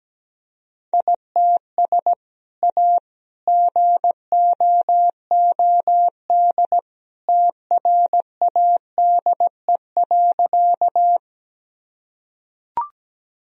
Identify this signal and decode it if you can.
Morse code, heard as a keyed tone, about 17 wpm, 700 Hz. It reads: ITS A GOOD TRADE.
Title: Sentences from Top 1000 Words - Review Part 4 - 17wpm